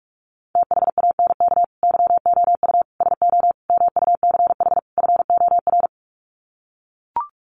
Morse code, heard as a keyed tone, about 44 wpm, 700 Hz. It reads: THANK YOU SO MUCH FOR
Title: Sets of 5 Words 45wpm